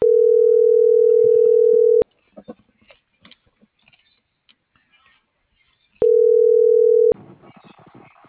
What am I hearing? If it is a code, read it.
Ringback tone (North America), 440 + 480 Hz.